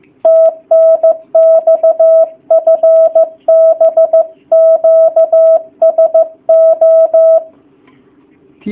Morse code, heard as a keyed tone, about 15 wpm, 650 Hz. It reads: TNX FB QSO
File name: jh8cbh / 2019cw / sp15- w